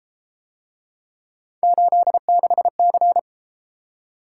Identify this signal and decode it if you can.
Morse code, heard as a keyed tone, about 32 wpm, 700 Hz. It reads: 86C